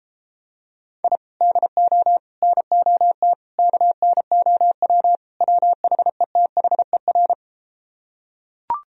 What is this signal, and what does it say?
Morse code, heard as a keyed tone, about 33 wpm, 700 Hz. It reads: I DO NOT KNOW WHETHER